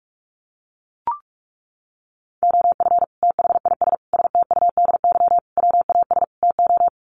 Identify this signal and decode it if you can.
Morse code, heard as a keyed tone, about 44 wpm, 700 Hz. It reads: OF THIS STUDY WAS TO
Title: Sets of 5 Words 45wpm